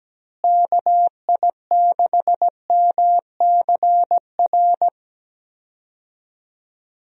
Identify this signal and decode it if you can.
Morse code, heard as a keyed tone, about 17 wpm, 700 Hz. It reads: KI6MCR
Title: USA Call Signs - 6 Characters - Repeated 2x - 17wpm